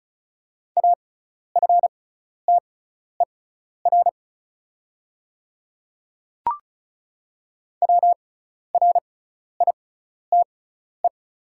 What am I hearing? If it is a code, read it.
Morse code, heard as a keyed tone, about 36 wpm, spaced as for 12 wpm, 700 Hz. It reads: AFTER WRITE
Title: 6x Character Spacing - Top 200 Words - 35wpm